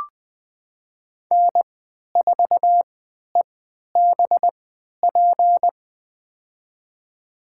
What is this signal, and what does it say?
Morse code, heard as a keyed tone, about 20 wpm, spaced as for 11 wpm, 700 Hz. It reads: N4EBP
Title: USA Call Signs - 5 Characters - 3x Character Spacing - 20wpm